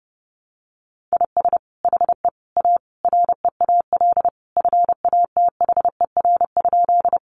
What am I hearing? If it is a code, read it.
Morse code, heard as a keyed tone, about 30 wpm, 700 Hz. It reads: IS HE A REAL FATHER?
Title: Sentences from Top 300 Words 30wpm